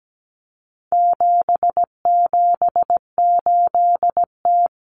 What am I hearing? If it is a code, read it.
Morse code, heard as a keyed tone, about 17 wpm, 700 Hz. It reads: 778T
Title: Common Cut Numbers - 4-Digit 17wpm